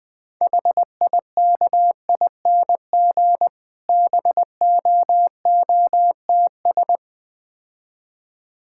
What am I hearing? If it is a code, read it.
Morse code, heard as a keyed tone, about 20 wpm, 700 Hz. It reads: HIKING BOOTS
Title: Difficult Words - Speed-Racing with Hints - 20wpm